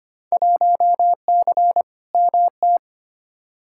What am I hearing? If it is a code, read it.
Morse code, heard as a keyed tone, about 25 wpm, 700 Hz. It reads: 1C MT
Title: Field Day Exchanges Encore - 25wpm